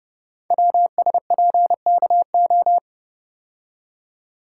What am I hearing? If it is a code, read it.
Morse code, heard as a keyed tone, about 30 wpm, 700 Hz. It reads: WSPKO
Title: Code Groups - 5 Letters - 30wpm